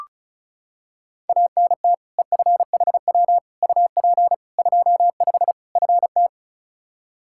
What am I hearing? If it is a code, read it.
Morse code, heard as a keyed tone, about 34 wpm, 700 Hz. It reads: ANT EFHW UP 25 FT
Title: K1USN SST QSO Practice 35wpm